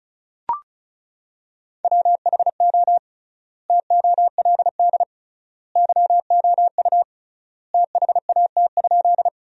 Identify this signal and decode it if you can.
Morse code, heard as a keyed tone, about 34 wpm, 700 Hz. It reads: WHO TOLD YOU THAT?
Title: 3x Word-Spacing - Sentences from Top 400 Words - 35wpm